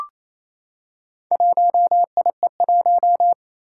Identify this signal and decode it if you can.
Morse code, heard as a keyed tone, about 28 wpm, 700 Hz. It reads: 1IE1
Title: Four Challenging Letter-Number Combinations 28wpm